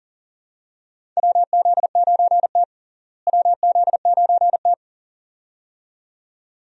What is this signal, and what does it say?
Morse code, heard as a keyed tone, about 37 wpm, 700 Hz. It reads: WZ9T WZ9T
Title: USA Call Signs - 4 Characters - Repeated 2x - 40wpm